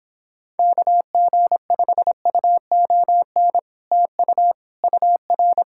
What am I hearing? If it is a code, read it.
Morse code, heard as a keyed tone, about 26 wpm, 700 Hz. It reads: KG5UON TU UR